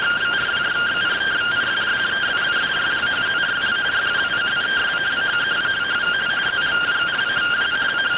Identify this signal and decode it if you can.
Two-tone teleprinter signal, about 170 Hz shift.